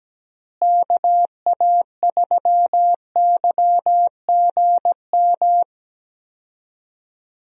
Morse code, heard as a keyed tone, about 17 wpm, 700 Hz. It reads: KA3YGM
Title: USA Call Signs - 6 Characters - Repeated 3x - 17wpm